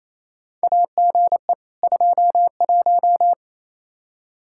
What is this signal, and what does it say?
Morse code, heard as a keyed tone, about 28 wpm, 700 Hz. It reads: AGE 21
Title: Practice QSO - Exchange 3 - More Info - 28wpm